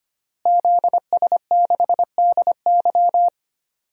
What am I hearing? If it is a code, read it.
Morse code, heard as a keyed tone, about 25 wpm, 700 Hz. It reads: ZS6DY
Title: DXCC-100 Call Signs - 5 Characters - Repeated 2x - 25wpm